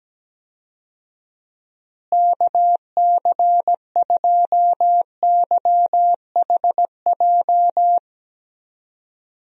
Morse code, heard as a keyed tone, about 17 wpm, 700 Hz. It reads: KC2YHJ
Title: USA Call Signs - 6 Characters - Repeated 3x - 17wpm